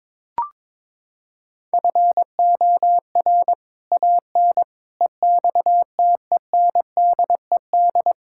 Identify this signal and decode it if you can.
Morse code, heard as a keyed tone, about 21 wpm, 700 Hz. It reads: FOR AN EXTENDED
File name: practice / 22wpm / Sets of 6 Words - 22wpm